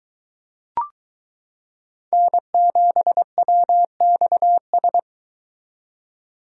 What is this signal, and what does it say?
Morse code, heard as a keyed tone, about 23 wpm, 700 Hz. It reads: N7WXS